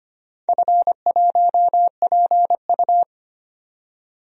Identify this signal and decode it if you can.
Morse code, heard as a keyed tone, about 25 wpm, 700 Hz. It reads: F1PU
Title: DXCC-100 Call Signs - 4 Characters - 25wpm